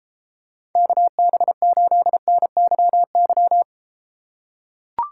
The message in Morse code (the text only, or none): KB8NYY